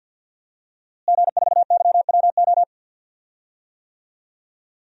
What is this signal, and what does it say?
Morse code, heard as a keyed tone, about 52 wpm, 700 Hz. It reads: G4YWO
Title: DXCC-100 Call Signs - 5 Characters - Repeated 3x - 50wpm